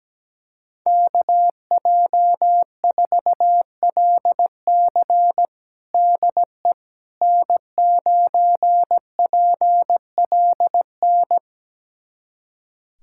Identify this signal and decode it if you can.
Morse code, heard as a keyed tone, about 17 wpm, 700 Hz. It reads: KJ4LC DE N9PLN
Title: USA Callsign DE Callsign - 5 Characters - 17wpm